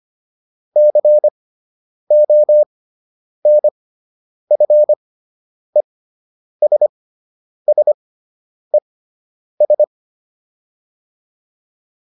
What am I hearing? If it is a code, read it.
Morse code, heard as a keyed tone, about 25 wpm, spaced as for 9 wpm, 600 Hz. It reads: CONFESSES